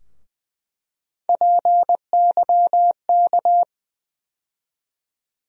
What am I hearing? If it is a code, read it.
Morse code, heard as a keyed tone, about 20 wpm, 700 Hz. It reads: PYK
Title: USA Call Sign Postfixes - 3 Letters 20wpm